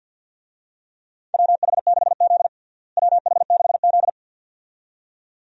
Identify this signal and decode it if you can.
Morse code, heard as a keyed tone, about 51 wpm, 700 Hz. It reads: WH6Z WH6Z
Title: USA Call Signs - 4 Characters - Repeated 2x - 50wpm